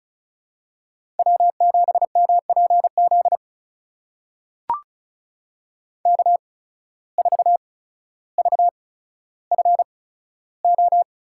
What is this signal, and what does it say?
Morse code, heard as a keyed tone, about 36 wpm, spaced as for 15 wpm, 700 Hz. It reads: W7MPZ K4VFO
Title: USA Call Signs - 5 Characters - 8x Character Spacing - 35wpm